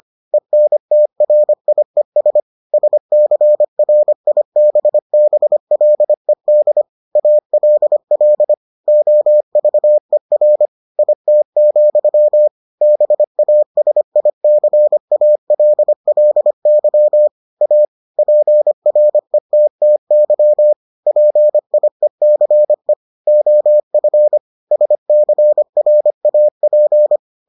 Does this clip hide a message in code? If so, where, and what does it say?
Yes: Morse code from 0.3 to 27.2 s, heard as a keyed tone, about 25 wpm, 600 Hz. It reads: ENTRIES SCRIBBLED ALL OVER IT, BASICALLY A PRETTY PIECE OF SCRAP